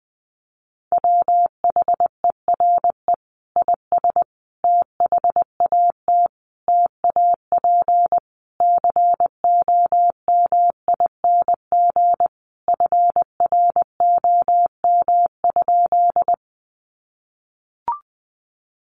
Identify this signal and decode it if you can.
Morse code, heard as a keyed tone, about 20 wpm, 700 Hz. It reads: WHERE IS THAT TAP COMING FROM?
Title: Sentences from Top 1400 Words 20wpm